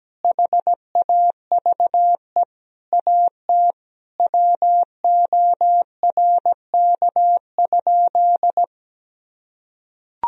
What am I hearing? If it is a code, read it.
Morse code, heard as a keyed tone, about 17 wpm, 700 Hz. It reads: HAVE AT WORK?
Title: Sentences from Top 1000 Words - Review Part 4 - 17wpm